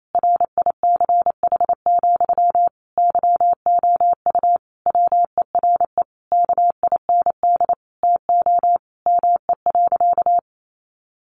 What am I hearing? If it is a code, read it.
Morse code, heard as a keyed tone, about 28 wpm, 700 Hz. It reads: RICH, YOU WERE KIND TO ME.
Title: Sentences from Top 1100 Words 28wpm